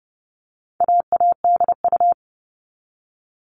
Morse code, heard as a keyed tone, about 30 wpm, 700 Hz. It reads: AADU
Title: All Cut Numbers - 4-Digit 30wpm